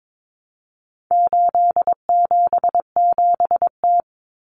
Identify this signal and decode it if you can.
Morse code, heard as a keyed tone, about 22 wpm, 700 Hz. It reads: 877T
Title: Common Cut Numbers - 4-Digit 22wpm